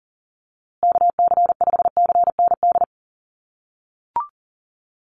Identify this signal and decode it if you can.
Morse code, heard as a keyed tone, about 40 wpm, 700 Hz. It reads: KC5CND
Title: USA Call Signs Repeated 2x 40wpm